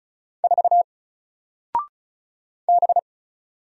Morse code, heard as a keyed tone, about 33 wpm, 700 Hz. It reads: 4 B
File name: Single-Letter-Number-35wpm